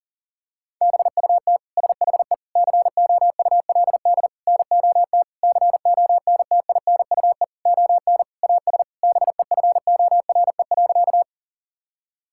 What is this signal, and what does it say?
Morse code, heard as a keyed tone, about 40 wpm, 700 Hz. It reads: BUT SHE COULD NOT CONTINUE ON AS BEFORE.
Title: Sentences from Top 1000 Words - Review Part 6 - 40wpm